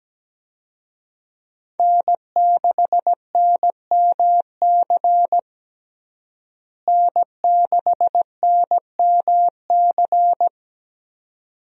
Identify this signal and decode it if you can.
Morse code, heard as a keyed tone, about 17 wpm, 700 Hz. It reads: N6NMC N6NMC
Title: USA Call Signs - 5 Characters - Repeated 2x - 17wpm